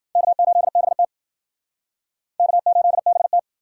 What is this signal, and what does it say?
Morse code, heard as a keyed tone, about 50 wpm, 700 Hz. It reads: K8BT K8BT